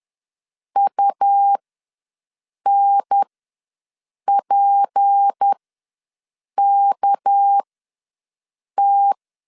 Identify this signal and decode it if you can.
Morse code, heard as a keyed tone, about 10 wpm, spaced as for 6 wpm, 800 Hz. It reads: UNPKT
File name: LEZIONI / 6 PKX0 / 6 PK ws 10wpm 3745